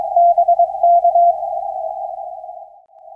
Morse code, heard as a keyed tone, about 22 wpm, 690 Hz. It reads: BK
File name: 1300 N1BUG 6m CW